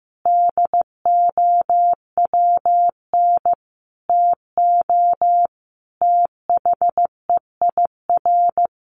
Morse code, heard as a keyed tone, about 15 wpm, 700 Hz. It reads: DOWN TO THEIR
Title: A Sound Of Thunder 15wpm